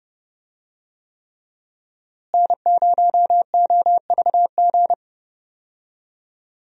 Morse code, heard as a keyed tone, about 30 wpm, 700 Hz. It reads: N0OVG
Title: USA Call Signs - 5 Characters - Repeated 3x - 30wpm